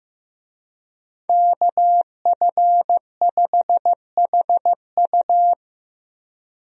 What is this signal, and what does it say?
Morse code, heard as a keyed tone, about 16 wpm, 700 Hz. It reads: KF5HU
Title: USA Call Signs - 5 Characters - Repeated 2x - 15wpm